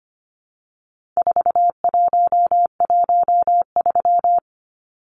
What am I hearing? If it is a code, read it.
Morse code, heard as a keyed tone, about 25 wpm, 700 Hz. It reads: 4113
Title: Numbers - 4-Digit 25wpm